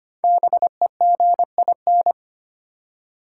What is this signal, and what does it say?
Morse code, heard as a keyed tone, about 25 wpm, 700 Hz. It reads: BEGIN
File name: 5 Letter Words - First 3 Letters Given - 25wpm